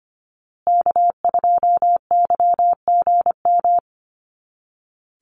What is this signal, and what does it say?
Morse code, heard as a keyed tone, about 25 wpm, 700 Hz. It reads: K2YGM
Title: Call Signs Repeated 2x - 25wpm